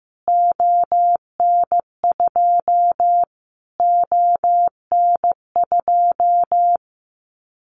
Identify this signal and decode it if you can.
Morse code, heard as a keyed tone, about 15 wpm, 700 Hz. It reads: ON2 ON2
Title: Callsign Prefixes for Top 25 DXCC - Repeated 3x - 15wpm